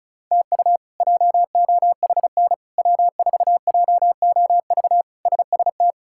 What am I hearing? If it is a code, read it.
Morse code, heard as a keyed tone, about 36 wpm, 700 Hz. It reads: TU JOHN W4JOV SST
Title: K1USN SST QSO Practice - 35wpm